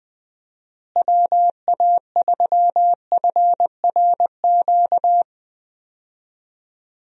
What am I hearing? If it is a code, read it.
Morse code, heard as a keyed tone, about 20 wpm, 700 Hz. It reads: WA3FRQ
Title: USA Call Signs - 6 Characters - Repeated 2x - 20wpm